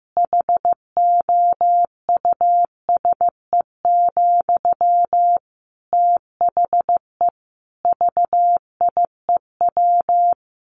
Morse code, heard as a keyed tone, about 15 wpm, 700 Hz. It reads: HOUSE, THE VIEW
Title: Sentences from Top 1100 Words 15wpm